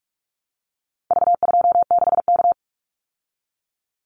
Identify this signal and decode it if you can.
Morse code, heard as a keyed tone, about 44 wpm, 700 Hz. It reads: V26K
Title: Call Signs Repeated 2x - 45wpm